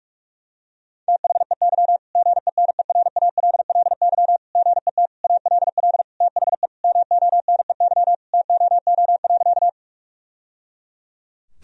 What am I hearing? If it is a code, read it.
Morse code, heard as a keyed tone, about 46 wpm, 700 Hz. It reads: THEY GENERALLY GET ALL THE MONEY TOO.
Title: Sentences from Top 1000 Words - Review Part 4 - 45wpm